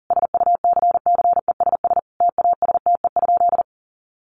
Morse code, heard as a keyed tone, about 40 wpm, 700 Hz. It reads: SUCCESS TASTE?
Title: Sentences from Top 1100 Words 40wpm